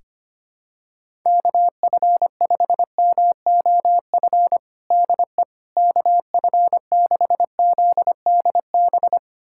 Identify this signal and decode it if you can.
Morse code, heard as a keyed tone, about 25 wpm, 700 Hz. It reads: KF5MOF DE KF6ZDB
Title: USA Callsign DE Callsign - 6 Characters - 25wpm